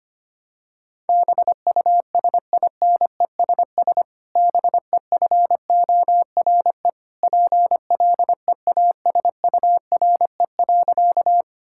Morse code, heard as a keyed tone, about 25 wpm, 700 Hz. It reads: BUSINESS BEFORE PLEASURE.
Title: Sentences from Top 1600 Words 25wpm